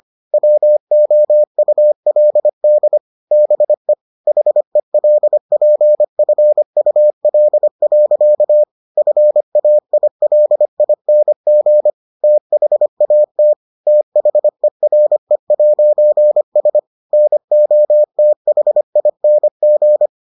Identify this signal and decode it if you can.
Morse code, heard as a keyed tone, about 25 wpm, 600 Hz. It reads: WOULD BE HELPFUL. FAILING THAT THERE'S NOTHING